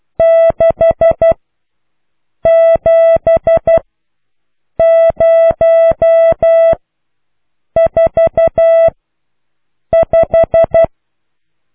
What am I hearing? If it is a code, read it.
Morse code, heard as a keyed tone, about 12 wpm, spaced as for 6 wpm, 660 Hz. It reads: 67045